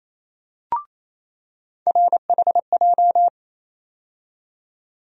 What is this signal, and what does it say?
Morse code, heard as a keyed tone, about 28 wpm, 700 Hz. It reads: RHJ
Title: USA Call Sign Postfixes - 3 Letters 28wpm